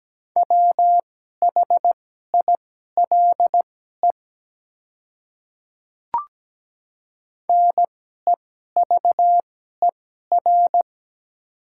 Morse code, heard as a keyed tone, about 17 wpm, spaced as for 12 wpm, 700 Hz. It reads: WHILE NEVER
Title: 2x Character Spacing - Top 300 Words - 17wpm